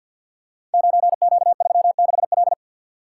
Morse code, heard as a keyed tone, about 48 wpm, 700 Hz. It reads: 9Q36L
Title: Code Groups - 5 Letter:Numbers - 50wpm